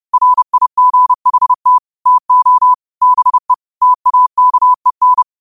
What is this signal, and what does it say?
Morse code, heard as a keyed tone, about 30 wpm, 1000 Hz. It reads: RIGHT TO BE TAKEN